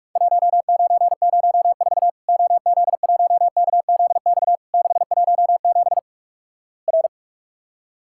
Morse code, heard as a keyed tone, about 48 wpm, 700 Hz. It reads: 1904 OZ1KZX 617 R
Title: Worked All Europe DX Contest - QTC Exchange - 45wpm